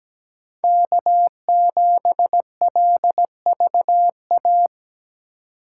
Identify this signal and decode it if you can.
Morse code, heard as a keyed tone, about 20 wpm, 700 Hz. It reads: K7LVA
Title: Speed Racing - USA 1x3 Call Signs 17wpm